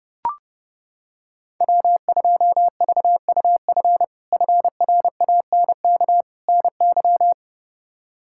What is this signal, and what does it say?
Morse code, heard as a keyed tone, about 30 wpm, 700 Hz. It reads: W2VUF FRANK NY
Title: K1USN SST QSO Practice - 30wpm